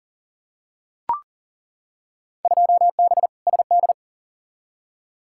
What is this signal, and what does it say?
Morse code, heard as a keyed tone, about 40 wpm, 700 Hz. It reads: 2B SD
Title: Field Day Exchanges Encore - 40wpm